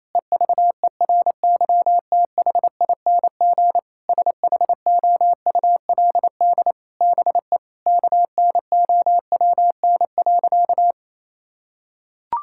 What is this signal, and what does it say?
Morse code, heard as a keyed tone, about 28 wpm, 700 Hz. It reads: EVERYTHING SHOULD BE KNOWN.
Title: Sentences from Top 1000 Words - Review Part 6 - 28wpm